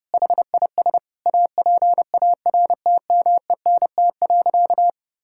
Morse code, heard as a keyed tone, about 30 wpm, 700 Hz. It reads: HIS APARTMENT.
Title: Sentences from Top 1000 Words - Review Part 6 - 30wpm